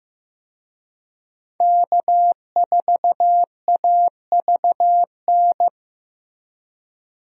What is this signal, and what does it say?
Morse code, heard as a keyed tone, about 15 wpm, 700 Hz. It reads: K4AVN
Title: USA Call Signs - 5 Characters - Repeated 3x - 15wpm